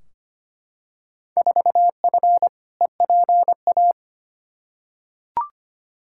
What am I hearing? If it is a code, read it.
Morse code, heard as a keyed tone, about 25 wpm, 700 Hz. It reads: 4F EPA
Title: Field Day Exchanges Encore - 25wpm